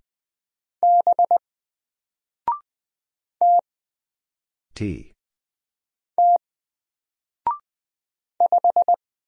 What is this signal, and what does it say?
Morse code, heard as a keyed tone, about 20 wpm, 700 Hz. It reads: B T T 5
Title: Single Letter-Number 20wpm